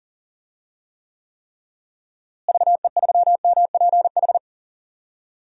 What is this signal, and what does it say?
Morse code, heard as a keyed tone, about 40 wpm, 700 Hz. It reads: VE3MPH
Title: DXCC-100 Call Signs - 6 Characters - Repeated 3x - 40wpm